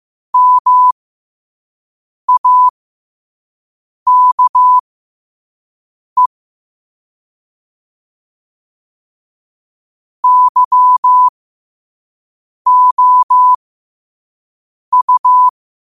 Morse code, heard as a keyed tone, about 15 wpm, spaced as for 5 wpm, 1000 Hz. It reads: MAKE YOU
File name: qotdfiles / 5wpm